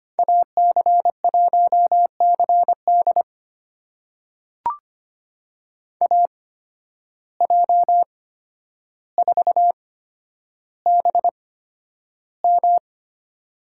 Morse code, heard as a keyed tone, about 25 wpm, spaced as for 11 wpm, 700 Hz. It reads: AC1CD AJ4BM
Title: USA Call Signs - 8x Character Spacing - 25wpm